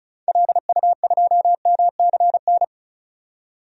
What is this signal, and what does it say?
Morse code, heard as a keyed tone, about 34 wpm, 700 Hz. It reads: LU2MCN
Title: DXCC-100 Call Signs - 6 Characters - 35wpm